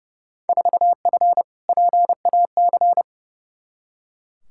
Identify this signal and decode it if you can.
Morse code, heard as a keyed tone, about 30 wpm, 700 Hz. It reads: 4F PAC
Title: Field Day Exchanges Encore - 30wpm